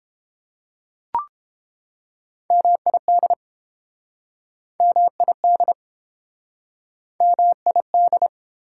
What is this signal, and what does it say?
Morse code, heard as a keyed tone, about 29 wpm, 700 Hz. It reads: MID MID MID